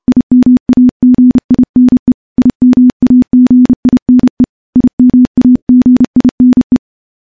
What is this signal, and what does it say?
Morse code, heard as a keyed tone, about 29 wpm, 250 Hz. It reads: IMAGINE IMAGINE IMAGINE
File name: morse / morse-training-30WPM